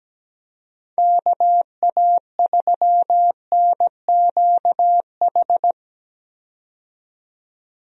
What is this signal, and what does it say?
Morse code, heard as a keyed tone, about 17 wpm, 700 Hz. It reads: KA3NQH
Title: USA Call Signs - 6 Characters - 17wpm